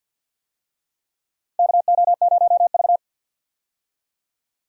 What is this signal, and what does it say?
Morse code, heard as a keyed tone, about 51 wpm, 700 Hz. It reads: KO0V
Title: USA Call Signs - 4 Characters - 50wpm